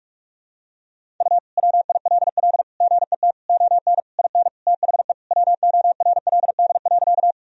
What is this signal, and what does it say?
Morse code, heard as a keyed tone, about 42 wpm, 700 Hz. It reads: U WILL GET ON IN THE WORLD.
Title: Sentences from Top 200 Words and Abbreviations 45wpm